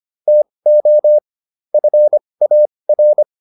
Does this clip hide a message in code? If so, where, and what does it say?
Yes: Morse code from 0.3 to 3.2 s, heard as a keyed tone, about 25 wpm, spaced as for 20 wpm, 600 Hz. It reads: TO FAR